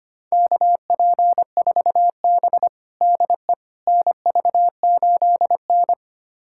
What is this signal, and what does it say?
Morse code, heard as a keyed tone, about 25 wpm, 700 Hz. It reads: KP4B DE NV8N
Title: USA Callsign DE Callsign - 4 Characters - 25wpm